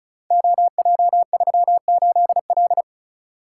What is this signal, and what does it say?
Morse code, heard as a keyed tone, about 36 wpm, 700 Hz. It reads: OJ38L
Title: Code Groups - 5 Letter:Numbers - 35wpm